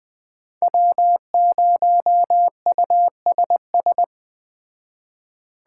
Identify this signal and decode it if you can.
Morse code, heard as a keyed tone, about 20 wpm, 700 Hz. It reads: W0USS